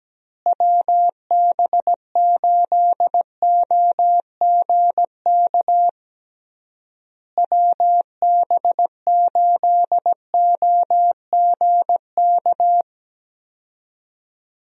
Morse code, heard as a keyed tone, about 17 wpm, 700 Hz. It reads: WB8OGK WB8OGK
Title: USA Call Signs - 6 Characters - Repeated 3x - 17wpm